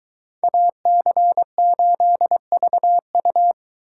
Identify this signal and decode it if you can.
Morse code, heard as a keyed tone, about 22 wpm, 700 Hz. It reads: AC8VU